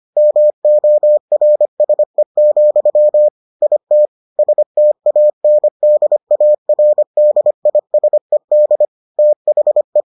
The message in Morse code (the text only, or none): MORSE, IT STANDARDISED THE